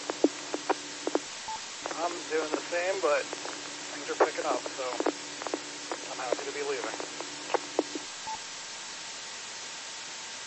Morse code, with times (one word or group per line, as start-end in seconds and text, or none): none